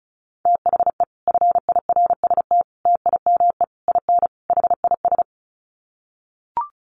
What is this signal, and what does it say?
Morse code, heard as a keyed tone, about 34 wpm, 700 Hz. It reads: THE FIRST TIME IN HIS
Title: Sets of 5 Words Encore2 35wpm